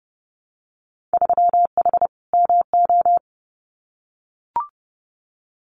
Morse code, heard as a keyed tone, about 30 wpm, 700 Hz. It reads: 3H MO
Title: Winter Field Day Exchanges Encore 30wpm